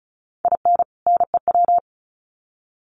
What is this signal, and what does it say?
Morse code, heard as a keyed tone, about 34 wpm, 700 Hz. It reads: IN NEW
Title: Sets of 2 Words 35wpm